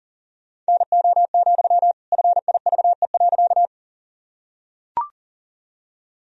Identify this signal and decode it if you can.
Morse code, heard as a keyed tone, about 37 wpm, 700 Hz. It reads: NO, FIVE.
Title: Sentences from Top 1000 Words - Review Part 2 - 40wpm